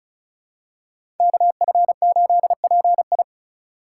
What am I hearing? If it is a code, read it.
Morse code, heard as a keyed tone, about 36 wpm, 700 Hz. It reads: KF8PI
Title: USA Call Signs - 5 Characters - 2x Character Spacing - 35wpm